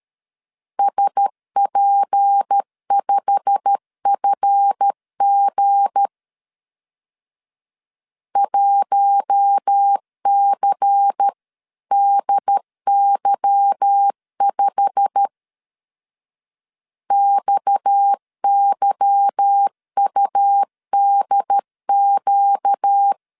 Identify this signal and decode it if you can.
Morse code, heard as a keyed tone, about 13 wpm, 800 Hz. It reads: SP5FG 1CDY5 XYUDQ
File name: CW / DET / 1 9 1500C S B 12wpm 3819